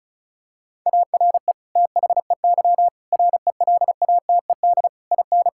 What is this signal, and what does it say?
Morse code, heard as a keyed tone, about 36 wpm, 700 Hz. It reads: ARE THEY RELATED IN
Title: Sentences from Top 1000 Words - Review Part 5 - 35wpm